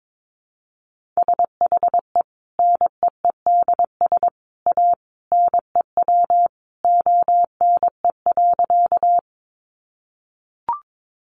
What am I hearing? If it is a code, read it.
Morse code, heard as a keyed tone, about 22 wpm, 700 Hz. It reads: SHE NEEDS A NEW ONE.
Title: Additional Easy Phrases 22wpm